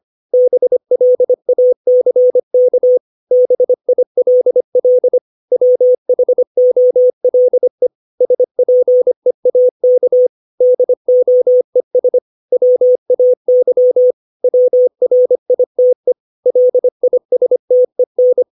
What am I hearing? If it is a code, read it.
Morse code, heard as a keyed tone, about 25 wpm, 500 Hz. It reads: BLACK BILL WHOLE SPEAK DOES WAY WRITE LISTEN
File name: 632 words / 632 Words 25WPM